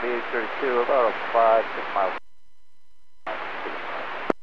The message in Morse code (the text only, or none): none